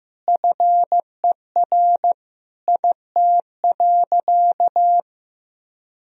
Morse code, heard as a keyed tone, about 15 wpm, 700 Hz. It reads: FER IT.
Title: Sentences from Top 200 Words and Abbreviations 15wpm